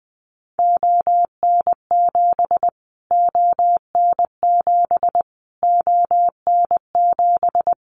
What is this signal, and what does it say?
Morse code, heard as a keyed tone, about 20 wpm, 700 Hz. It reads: ON7 ON7 ON7
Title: Callsign Prefixes for Top 25 DXCC - Repeated 3x - 20wpm